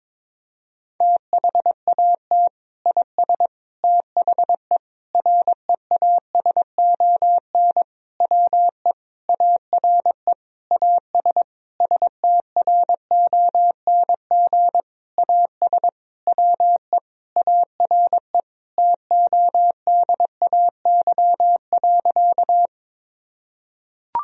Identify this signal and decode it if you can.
Morse code, heard as a keyed tone, about 22 wpm, 700 Hz. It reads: THAT IS THE REASON WE ARE AS STRONG AS WE ARE TODAY.
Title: Sentences from Top 1000 Words - Review Part 4 - 22wpm